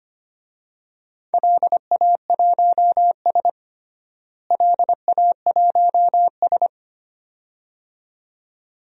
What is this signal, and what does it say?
Morse code, heard as a keyed tone, about 25 wpm, 700 Hz. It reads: LA1S LA1S
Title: DXCC-100 Call Signs - 4 Characters - Repeated 2x - 25wpm